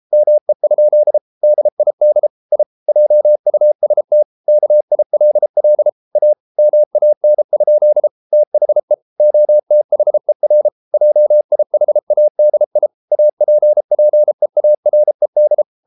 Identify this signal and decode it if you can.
Morse code, heard as a keyed tone, about 33 wpm, 600 Hz. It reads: ME? DID I JUST KILL A MAN? THE OTHER JIHADI APPEARED